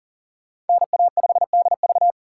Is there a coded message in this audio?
Yes: Morse code from 0.7 to 2.1 s, heard as a keyed tone, about 51 wpm, 700 Hz. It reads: NA5DV